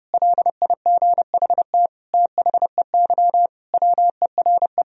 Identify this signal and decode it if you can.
Morse code, heard as a keyed tone, about 30 wpm, 700 Hz. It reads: LIGHT THEY WERE
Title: Sentences from Top 1000 Words - Review Part 1 - 30wpm